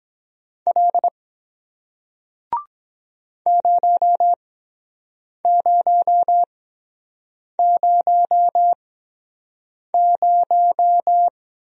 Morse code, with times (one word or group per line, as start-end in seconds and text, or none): 0.7-1.1 L
3.5-11.3 0000